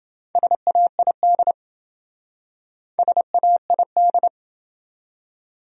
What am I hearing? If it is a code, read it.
Morse code, heard as a keyed tone, about 28 wpm, 700 Hz. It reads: SAID SAID